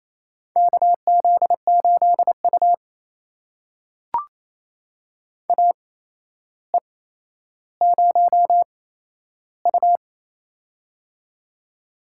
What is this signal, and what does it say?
Morse code, heard as a keyed tone, about 28 wpm, spaced as for 12 wpm, 700 Hz. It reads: KZ8U AE0U
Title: USA Call Signs - 4 Characters - 8x Character Spacing - 28wpm